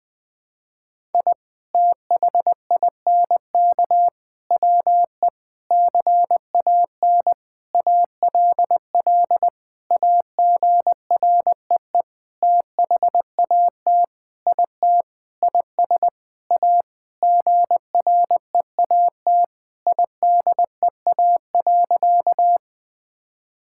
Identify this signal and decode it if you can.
Morse code, heard as a keyed tone, about 19 wpm, 700 Hz. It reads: I THINK WE CAN ALL AGREE THAT IT IS A GREAT IDEA.
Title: Sentences with 7-Word Phrases - 20wpm